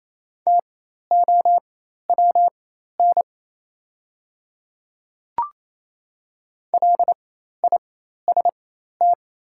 Morse code, heard as a keyed tone, about 28 wpm, spaced as for 13 wpm, 690 Hz. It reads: TOWN LIST
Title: 4x Character Spacing - Top 500 Words - 28wpm